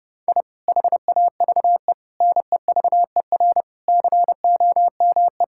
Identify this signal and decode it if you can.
Morse code, heard as a keyed tone, about 32 wpm, 700 Hz. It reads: I HAVE NEVER COME
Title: Sentences from Top 1000 Words - Review Part 5 - 30wpm